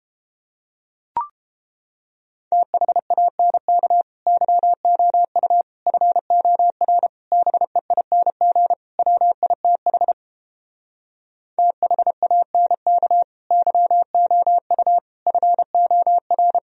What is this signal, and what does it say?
Morse code, heard as a keyed tone, about 29 wpm, 700 Hz. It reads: THANK YOU FOR BEING WITH THANK YOU FOR